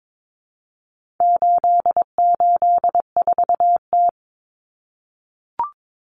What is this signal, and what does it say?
Morse code, heard as a keyed tone, about 22 wpm, 700 Hz. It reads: 884T
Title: Common Cut Numbers - 4-Digit 22wpm